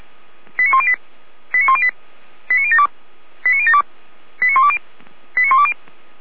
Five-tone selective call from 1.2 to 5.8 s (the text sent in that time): 89108 89072 89072 89120 89120